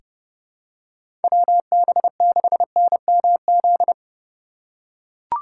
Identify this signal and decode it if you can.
Morse code, heard as a keyed tone, about 32 wpm, 700 Hz. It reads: WB6NMZ
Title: USA Call Signs - 6 Characters - 4x Character Spacing - 30wpm